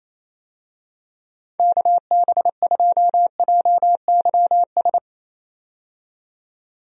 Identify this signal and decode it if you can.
Morse code, heard as a keyed tone, about 28 wpm, 700 Hz. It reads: KB2JYS
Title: USA Call Signs - 6 Characters - 28wpm